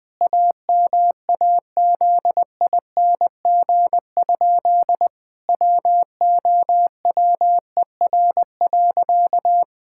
Morse code, heard as a keyed tone, about 20 wpm, 700 Hz. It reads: AMAZING? WOWER.
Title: Pangram Challenge v2 - 20wpm